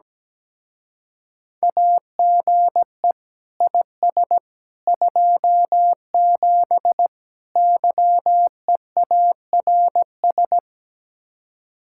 Morse code, heard as a keyed tone, about 17 wpm, 700 Hz. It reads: AGE IS 27 YEARS
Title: Speed-Racing Common QSO Phrases - 17wpm